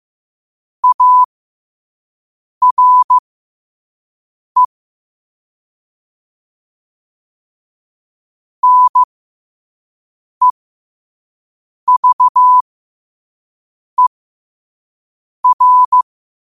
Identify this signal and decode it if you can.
Morse code, heard as a keyed tone, about 15 wpm, spaced as for 5 wpm, 1000 Hz. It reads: ARE NEVER
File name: qotdfiles / 5wpm